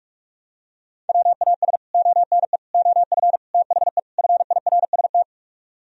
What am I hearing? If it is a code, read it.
Morse code, heard as a keyed tone, about 45 wpm, 700 Hz. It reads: WAS ONE OF THE FIRST